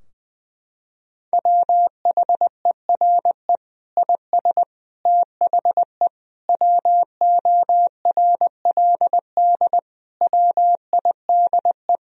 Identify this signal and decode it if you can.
Morse code, heard as a keyed tone, about 20 wpm, 700 Hz. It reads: WHERE IS THE WORLD WIDE
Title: Sentences from Top 1000 Words - Review Part 4 - 20wpm